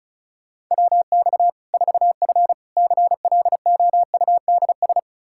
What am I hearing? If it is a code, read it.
Morse code, heard as a keyed tone, about 36 wpm, 700 Hz. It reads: WX 4F CLOUDS
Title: Practice QSO - Exchange 2 - Info - 35wpm